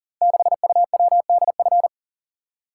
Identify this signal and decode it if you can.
Morse code, heard as a keyed tone, about 39 wpm, 690 Hz. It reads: 6UWDF
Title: Code Groups - 5 Letter:Numbers - 40wpm